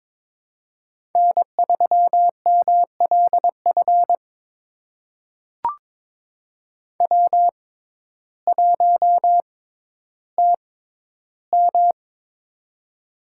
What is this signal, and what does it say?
Morse code, heard as a keyed tone, about 21 wpm, spaced as for 12 wpm, 700 Hz. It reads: N3MLF W1TM
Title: USA Call Signs - 6x Character Spacing - 22wpm